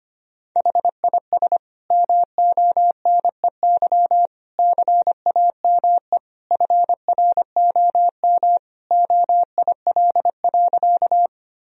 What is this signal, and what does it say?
Morse code, heard as a keyed tone, about 25 wpm, 700 Hz. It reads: HIS MONEY CAME FROM OIL.
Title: Sentences from Top 1000 Words - Review Part 4 - 25wpm